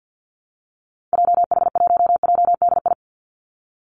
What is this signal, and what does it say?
Morse code, heard as a keyed tone, about 50 wpm, 700 Hz. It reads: JH1JDI
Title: Call Signs - 50wpm